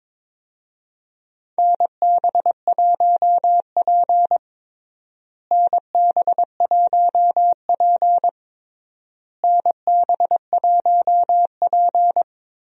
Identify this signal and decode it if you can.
Morse code, heard as a keyed tone, about 22 wpm, 700 Hz. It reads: NB1P NB1P NB1P
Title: USA Call Signs - 4 Characters - Repeated 3x - 22wpm